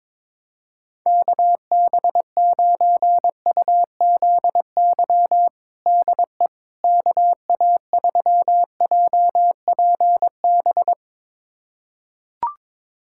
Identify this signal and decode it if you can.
Morse code, heard as a keyed tone, about 22 wpm, 700 Hz. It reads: KB9UZY DE KA3JPB
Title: USA Callsign DE Callsign - 6 Characters - 22wpm